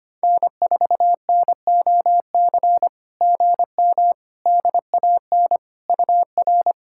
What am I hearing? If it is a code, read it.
Morse code, heard as a keyed tone, about 25 wpm, 700 Hz. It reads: N4NOC GM DAN UR
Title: SOTA QSO Practice 25wpm